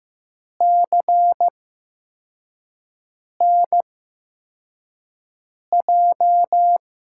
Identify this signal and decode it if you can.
Morse code, heard as a keyed tone, about 15 wpm, 700 Hz. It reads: C N J
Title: USA Call Signs - 6 Characters - 8x Character Spacing - 15wpm